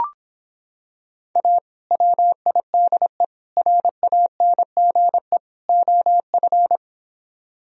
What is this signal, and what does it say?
Morse code, heard as a keyed tone, about 26 wpm, 700 Hz. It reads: A WIDE RANGE OF